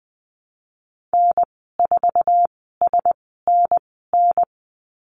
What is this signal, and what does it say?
Morse code, heard as a keyed tone, about 20 wpm, spaced as for 15 wpm, 700 Hz. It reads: N4SNN
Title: 048 Lesson 9g - Number 4 - Call Signs 20wpm